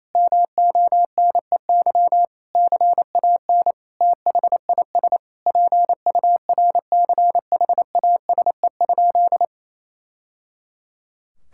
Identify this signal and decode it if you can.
Morse code, heard as a keyed tone, about 28 wpm, 700 Hz. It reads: MONEY CAN THIS PURCHASE?
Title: Sentences from Top 1000 Words - Review Part 5 - 28wpm